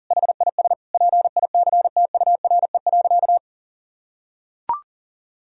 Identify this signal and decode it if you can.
Morse code, heard as a keyed tone, about 38 wpm, 700 Hz. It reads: HIS PICTURE.
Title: Sentences from Top 1000 Words - Review Part 1 - 40wpm